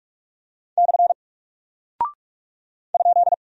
Morse code, heard as a keyed tone, about 44 wpm, 700 Hz. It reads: / ?
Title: Single Character (Letter:Number:Punctuation:Prosign) Alt 45wpm